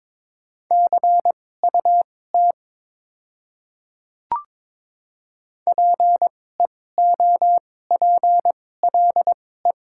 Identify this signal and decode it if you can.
Morse code, heard as a keyed tone, about 22 wpm, spaced as for 16 wpm, 700 Hz. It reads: CUT PEOPLE
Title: 2x Character Spacing - Top 300 Words - 22wpm